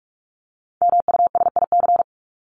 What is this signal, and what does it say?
Morse code, heard as a keyed tone, about 44 wpm, 700 Hz. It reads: MUSIC
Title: Top 300 Words 45wpm